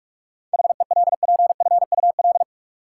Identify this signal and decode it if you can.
Morse code, heard as a keyed tone, about 46 wpm, 700 Hz. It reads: HELPFUL
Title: Seven Letter Words - 45wpm